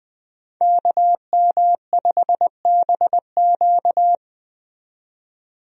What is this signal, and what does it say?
Morse code, heard as a keyed tone, about 20 wpm, 700 Hz. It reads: KM5BQ
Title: USA Call Signs - 5 Characters - 20wpm